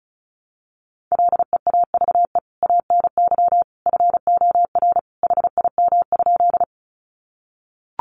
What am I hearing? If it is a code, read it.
Morse code, heard as a keyed tone, about 34 wpm, 700 Hz. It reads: LEAVE ANY FOR HIM?
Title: Sentences from Top 400 Words 35wpm